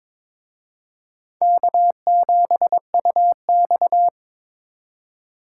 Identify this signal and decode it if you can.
Morse code, heard as a keyed tone, about 26 wpm, 700 Hz. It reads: K7UX
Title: Speed Racing - USA 1x2 Call Signs 22wpm